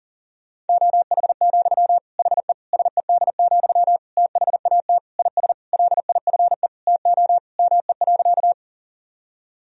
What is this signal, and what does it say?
Morse code, heard as a keyed tone, about 40 wpm, 700 Hz. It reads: OH, HE SED, THAT IS LIFE TO ME.
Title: Sentences from Top 500 Words and Abbreviations 40wpm